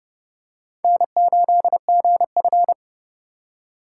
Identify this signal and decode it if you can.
Morse code, heard as a keyed tone, about 28 wpm, 700 Hz. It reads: N8GF